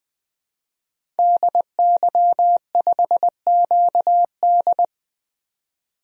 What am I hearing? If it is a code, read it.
Morse code, heard as a keyed tone, about 21 wpm, 700 Hz. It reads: DY5QD
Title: DXCC-100 Call Signs - 5 Characters - 20wpm